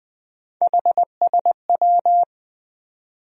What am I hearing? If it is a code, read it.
Morse code, heard as a keyed tone, about 20 wpm, 700 Hz. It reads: HSW
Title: USA Call Sign Postfixes - 3 Letters 20wpm